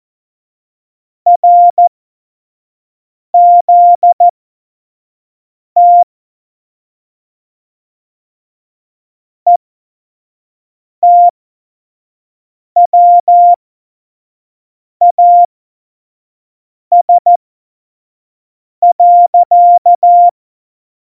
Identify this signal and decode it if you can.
Morse code, heard as a keyed tone, about 14 wpm, spaced as for 5 wpm, 700 Hz. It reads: RZT ETWAS.